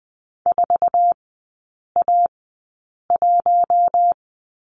Morse code, heard as a keyed tone, about 20 wpm, spaced as for 8 wpm, 700 Hz. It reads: 4A1
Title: 047 Lesson 9e - Number 4 - Three Characters 20wpm